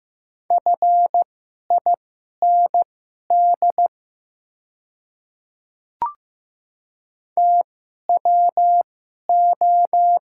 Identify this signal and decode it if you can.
Morse code, heard as a keyed tone, about 15 wpm, spaced as for 11 wpm, 700 Hz. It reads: FIND TWO
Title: 2x Character Spacing - Top 300 Words - 15wpm